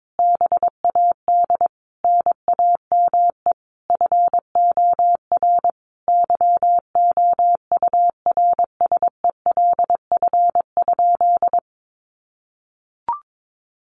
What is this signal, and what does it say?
Morse code, heard as a keyed tone, about 22 wpm, 700 Hz. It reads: BAD NAME FOR YOURSELF?
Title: Sentences from Top 1500 Words 22wpm